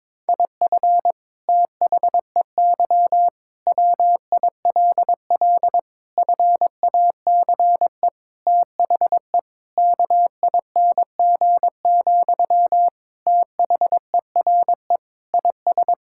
Morse code, heard as a keyed tone, about 22 wpm, 700 Hz. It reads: IF THEY WILL FACE THE KING, THERE IS
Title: Sentences from Top 1000 Words - Review Part 6 - 22wpm